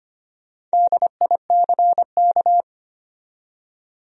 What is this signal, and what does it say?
Morse code, heard as a keyed tone, about 25 wpm, 700 Hz. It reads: DICK
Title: USA Common Names - Max 5 Letters - 25wpm